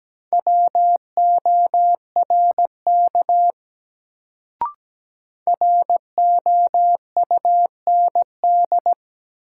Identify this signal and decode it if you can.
Morse code, heard as a keyed tone, about 17 wpm, 700 Hz. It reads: WORK ROUND
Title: Sentences from Top 200 Words - New Word Drill - 17wpm